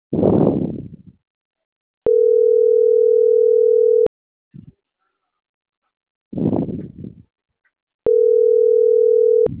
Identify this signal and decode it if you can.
Ringback tone (North America), 440 + 480 Hz.